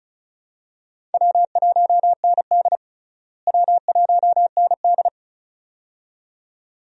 Morse code, heard as a keyed tone, about 36 wpm, 700 Hz. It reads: W1ND W1ND
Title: USA Call Signs - 4 Characters - Repeated 2x - 35wpm